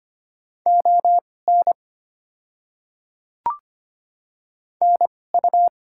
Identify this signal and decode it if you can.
Morse code, heard as a keyed tone, about 25 wpm, spaced as for 6 wpm, 690 Hz. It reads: ONNU
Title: 2x Character Spacing - Top 100 Words - 25wpm